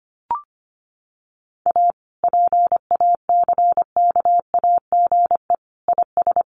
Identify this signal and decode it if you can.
Morse code, heard as a keyed tone, about 25 wpm, 700 Hz. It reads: A PACKAGE IS
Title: Additional Easy Phrases 25wpm